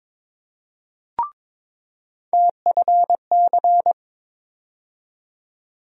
Morse code, heard as a keyed tone, about 22 wpm, 700 Hz. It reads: TFC
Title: USA Call Sign Postfixes - 3 Letters 22wpm